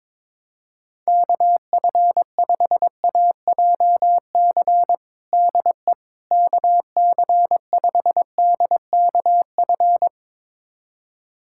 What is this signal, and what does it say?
Morse code, heard as a keyed tone, about 22 wpm, 700 Hz. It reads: KF5AJC DE KC5DKF
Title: USA Callsign DE Callsign - 6 Characters - 22wpm